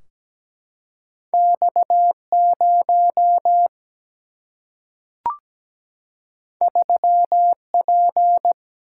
Morse code, heard as a keyed tone, about 17 wpm, 700 Hz. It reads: X 0 3 P
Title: Single-Letter Single-Number 17wpm